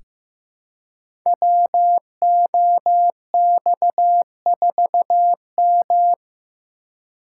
Morse code, heard as a keyed tone, about 15 wpm, 700 Hz. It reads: WOX4M
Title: Code Groups - 5 Letter:Numbers - 15wpm